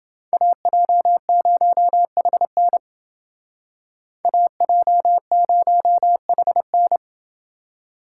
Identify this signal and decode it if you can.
Morse code, heard as a keyed tone, about 29 wpm, 700 Hz. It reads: AJ0HN AJ0HN